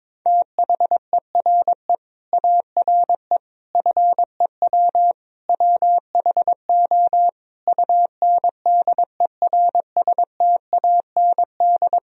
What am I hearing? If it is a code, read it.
Morse code, heard as a keyed tone, about 22 wpm, 700 Hz. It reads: THERE ARE FEW WHO UNDERSTAND
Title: Sentences from Top 1000 Words - Review Part 1 - 22wpm